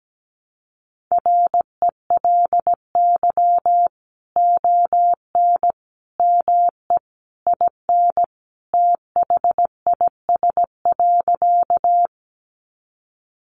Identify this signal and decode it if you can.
Morse code, heard as a keyed tone, about 17 wpm, 700 Hz. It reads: RELY ON ME IN THIS.
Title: Sentences from Top 1200 Words 17wpm